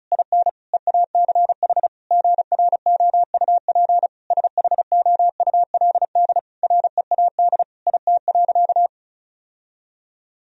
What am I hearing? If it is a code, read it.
Morse code, heard as a keyed tone, about 36 wpm, 700 Hz. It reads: IN EACH GROUP SHOULD READ IT.
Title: Sentences from Top 1000 Words - Review Part 1 - 35wpm